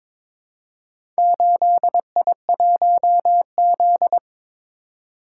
Morse code, heard as a keyed tone, about 22 wpm, 700 Hz. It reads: 8I1Z
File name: practice / 22wpm / Four Letter-Number Combinations - 22wpm